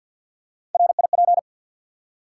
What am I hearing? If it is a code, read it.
Morse code, heard as a keyed tone, about 52 wpm, 700 Hz. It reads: RIP
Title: Three Letter Words - 50wpm